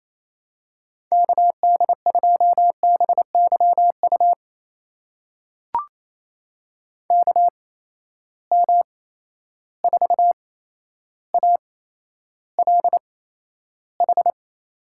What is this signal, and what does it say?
Morse code, heard as a keyed tone, about 28 wpm, spaced as for 12 wpm, 700 Hz. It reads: KD2BYU KM4ALH
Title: USA Call Signs - 6 Characters - 8x Character Spacing - 28wpm